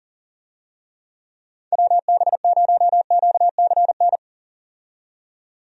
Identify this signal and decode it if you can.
Morse code, heard as a keyed tone, about 40 wpm, 700 Hz. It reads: WB0QCN
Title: USA Call Signs - 6 Characters - 40wpm